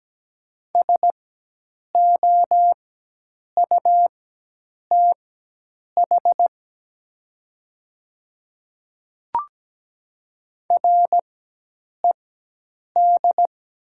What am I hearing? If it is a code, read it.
Morse code, heard as a keyed tone, about 17 wpm, spaced as for 8 wpm, 700 Hz. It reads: SOUTH RED
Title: 4x Character Spacing - Top 500 Words - 17wpm